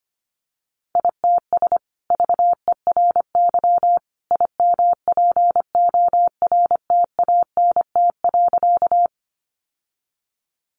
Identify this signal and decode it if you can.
Morse code, heard as a keyed tone, about 25 wpm, 700 Hz. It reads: ITS VERY IMPORTANT.
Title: Sentences from Top 600 Words 25wpm